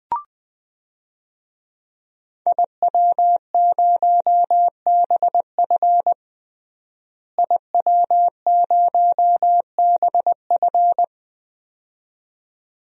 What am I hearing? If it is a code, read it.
Morse code, heard as a keyed tone, about 20 wpm, 700 Hz. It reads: IW0BF IW0BF
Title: DXCC-100 Call Signs - 5 Characters - Repeated 2x - 20wpm